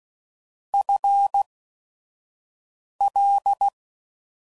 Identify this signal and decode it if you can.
Morse code, heard as a keyed tone, about 16 wpm, spaced as for 5 wpm, 800 Hz. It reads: FL
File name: Prefixes / FLs & THs / FL Prefixes 1